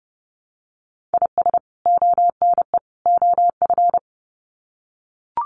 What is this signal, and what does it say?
Morse code, heard as a keyed tone, about 30 wpm, 700 Hz. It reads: IS ONE OF
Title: Sets of 3 Words 30wpm